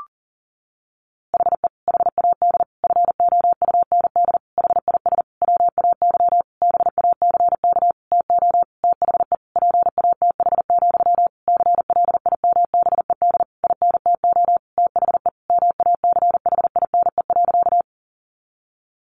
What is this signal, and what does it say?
Morse code, heard as a keyed tone, about 40 wpm, 700 Hz. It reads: HE HAD FOUND HIS WAY BACK TO THE PATH, CLIMBED INTO THE MACHINE.
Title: A Sound Of Thunder 40wpm